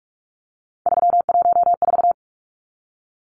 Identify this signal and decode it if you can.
Morse code, heard as a keyed tone, about 44 wpm, 700 Hz. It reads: 314
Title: Numbers - 3-Digit 45wpm